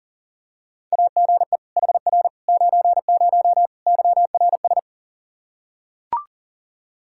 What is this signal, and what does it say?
Morse code, heard as a keyed tone, about 40 wpm, 700 Hz. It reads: AGE HR 90 YRS
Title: Practice QSO - Exchange 3 - More Info - 40wpm